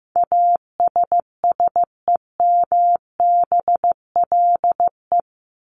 Morse code, heard as a keyed tone, about 15 wpm, 700 Hz. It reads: ASSEMBLE
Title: LY Suffix - Top 100 Words with Base Words 15wpm